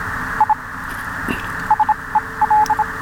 Morse code, heard as a keyed tone, about 26 wpm, spaced as for 21 wpm, 850 Hz, with radio noise. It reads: I SEL